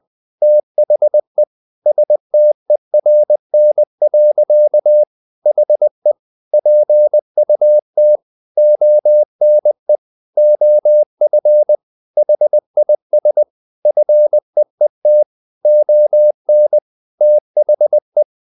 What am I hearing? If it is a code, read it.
Morse code, heard as a keyed tone, about 20 wpm, 600 Hz. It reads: THE STERN. HE PUT ONE OF HIS FEET ON THE
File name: the old man and the sea part 2 20wpm